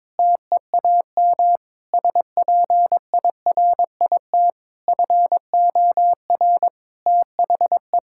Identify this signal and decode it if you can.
Morse code, heard as a keyed tone, about 22 wpm, 700 Hz. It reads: TEAM SPIRIT FOR THE
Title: Sentences from Top 1600 Words 22wpm